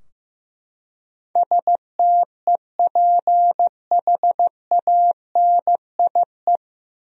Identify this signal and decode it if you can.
Morse code, heard as a keyed tone, about 15 wpm, 700 Hz. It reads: STEPHANIE
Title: USA Common Names - 15wpm